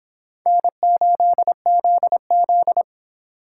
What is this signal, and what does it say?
Morse code, heard as a keyed tone, about 29 wpm, 700 Hz. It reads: N8ZZ